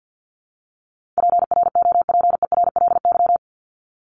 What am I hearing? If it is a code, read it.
Morse code, heard as a keyed tone, about 50 wpm, 700 Hz. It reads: PROPERLY
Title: LY Suffix - Top 100 Words with Base Words 50wpm